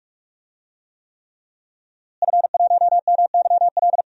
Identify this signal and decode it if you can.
Morse code, heard as a keyed tone, about 41 wpm, 700 Hz. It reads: F1MYL
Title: DXCC-100 Call Signs - 5 Characters - Repeated 3x - 45wpm